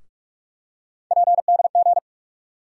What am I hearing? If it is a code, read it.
Morse code, heard as a keyed tone, about 42 wpm, 700 Hz. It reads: PDG
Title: USA Call Sign Postfixes - 3 Letters 45wpm